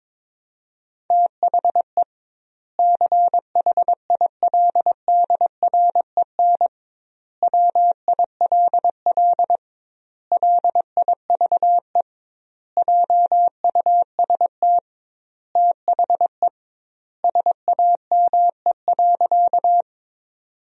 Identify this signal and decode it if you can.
Morse code, heard as a keyed tone, about 21 wpm, 700 Hz. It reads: THE CHILDREN WILL LIVE JUST THE SAME.
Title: 2x Word-Spacing - Sentences from Top 300 Words - 22wpm